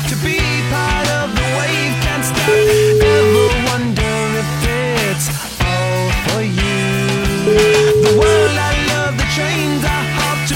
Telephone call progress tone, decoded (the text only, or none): ringback tone